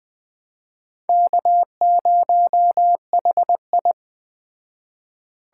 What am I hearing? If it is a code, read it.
Morse code, heard as a keyed tone, about 18 wpm, 700 Hz. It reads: K0HI
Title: Speed Racing - USA 1x2 Call Signs 15wpm